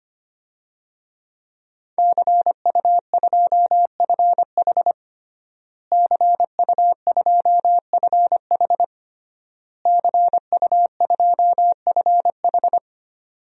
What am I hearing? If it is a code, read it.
Morse code, heard as a keyed tone, about 25 wpm, 700 Hz. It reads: CU2FH CU2FH CU2FH
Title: DXCC-100 Call Signs - 5 Characters - Repeated 3x - 25wpm